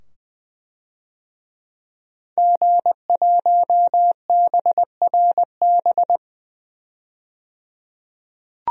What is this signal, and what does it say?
Morse code, heard as a keyed tone, about 20 wpm, 700 Hz. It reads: G1BRB
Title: DXCC-100 Call Signs - 5 Characters - 20wpm